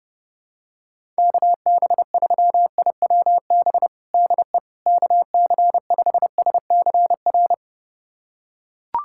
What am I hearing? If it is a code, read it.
Morse code, heard as a keyed tone, about 32 wpm, 700 Hz. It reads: KB3IWB DE KC5SCR
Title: USA Callsign DE Callsign - 6 Characters - 30wpm